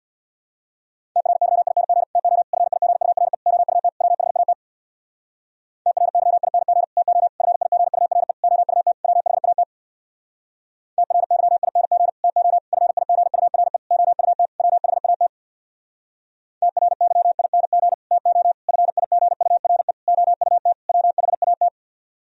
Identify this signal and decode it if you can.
Morse code, heard as a keyed tone, about 64 wpm, 700 Hz. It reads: TRYING TO FIGURE OUT WHAT TRYING TO FIGURE OUT WHAT TRYING TO FIGURE OUT WHAT TRYING TO FIGURE OUT WHAT